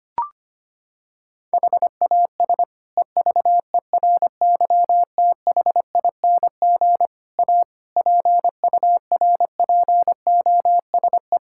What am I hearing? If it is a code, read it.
Morse code, heard as a keyed tone, about 25 wpm, 700 Hz. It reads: HAS EVERYTHING A PURPOSE
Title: Sentences from Top 1600 Words 25wpm